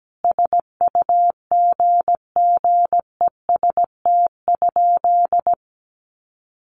Morse code, heard as a keyed tone, about 17 wpm, 700 Hz. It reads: SUGGEST?
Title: Sentences from Top 1300 Words 17wpm